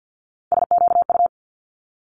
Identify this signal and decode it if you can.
Morse code, heard as a keyed tone, about 50 wpm, 700 Hz. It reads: SQU
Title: Multi-letter Phonemes Quick Fire 50wpm